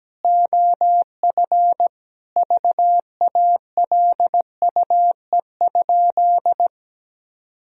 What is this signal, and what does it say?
Morse code, heard as a keyed tone, about 17 wpm, 700 Hz. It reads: OF VALUE?
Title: Sentences from Top 1000 Words - Review Part 3 - 17wpm